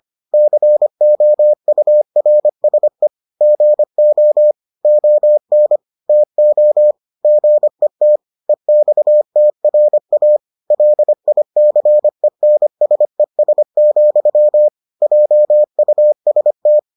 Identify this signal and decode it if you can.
Morse code, heard as a keyed tone, about 25 wpm, 600 Hz. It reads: COURSE GO ON TO GET EXTRA LICENSES, JUST